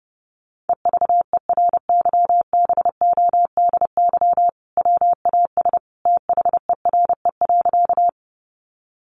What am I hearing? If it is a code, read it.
Morse code, heard as a keyed tone, about 30 wpm, 700 Hz. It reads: EVERYBODY WAS THERE.
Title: Sentences from Top 1400 Words 30wpm